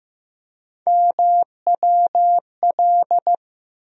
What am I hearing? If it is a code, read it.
Morse code, heard as a keyed tone, about 15 wpm, 700 Hz. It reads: MWL
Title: USA Call Sign Postfixes - 3 Letters 15wpm